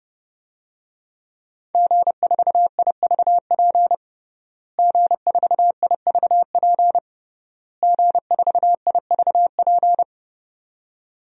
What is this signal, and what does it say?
Morse code, heard as a keyed tone, about 30 wpm, 700 Hz. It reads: G4IVP G4IVP G4IVP
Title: DXCC-100 Call Signs - 5 Characters - Repeated 3x - 30wpm